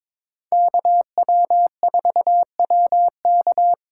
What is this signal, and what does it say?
Morse code, heard as a keyed tone, about 22 wpm, 700 Hz. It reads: KW4WK
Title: USA Call Signs - 4x Character Spacing - 22wpm